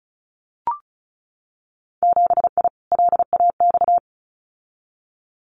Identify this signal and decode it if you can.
Morse code, heard as a keyed tone, about 36 wpm, 700 Hz. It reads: 7I LAX
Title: Winter Field Day Exchanges Encore 35wpm